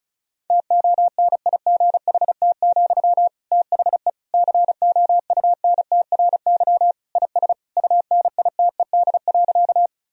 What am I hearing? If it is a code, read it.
Morse code, heard as a keyed tone, about 36 wpm, 700 Hz. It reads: TONIGHT, THE COUNTRY IS UNITED.
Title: Sentences from Top 1000 Words - Review Part 4 - 35wpm